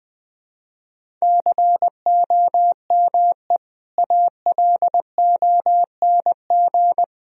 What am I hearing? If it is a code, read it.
Morse code, heard as a keyed tone, about 21 wpm, 700 Hz. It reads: COME ALONG
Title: Sentences from Top 1000 Words - Review Part 4 - 20wpm